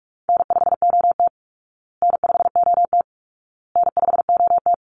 Morse code, heard as a keyed tone, about 44 wpm, 700 Hz. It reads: N5OT N5OT N5OT
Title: Easy Call Signs Repeated 3x - 45wpm